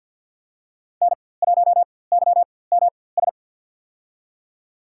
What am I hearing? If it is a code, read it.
Morse code, heard as a keyed tone, about 46 wpm, spaced as for 23 wpm, 700 Hz. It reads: N1YMS
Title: USA Call Signs - 5 Characters - 4x Character Spacing - 50wpm